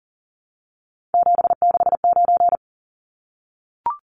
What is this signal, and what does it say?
Morse code, heard as a keyed tone, about 40 wpm, 700 Hz. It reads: 769
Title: Numbers - 3-Digit 40wpm